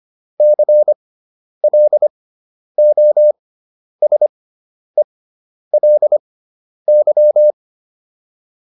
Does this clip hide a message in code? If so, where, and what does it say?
Yes: Morse code from 0.4 to 7.5 s, heard as a keyed tone, about 25 wpm, spaced as for 10 wpm, 600 Hz. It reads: CLOSELY